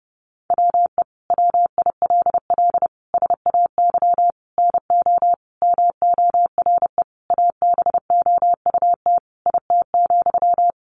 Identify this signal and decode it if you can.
Morse code, heard as a keyed tone, about 30 wpm, 700 Hz. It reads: WE WILL SAY NO MORE ABOUT IT,
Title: Sentences from Top 200 Words 30wpm